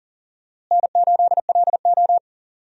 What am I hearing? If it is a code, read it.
Morse code, heard as a keyed tone, about 40 wpm, 700 Hz. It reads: N8LO
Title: USA Call Signs - 4 Characters - Repeated 2x - 40wpm